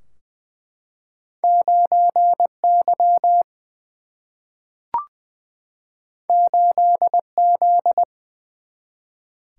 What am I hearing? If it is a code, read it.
Morse code, heard as a keyed tone, about 20 wpm, 700 Hz. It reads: 9 Y 8 Z
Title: Single-Letter Single-Number 20wpm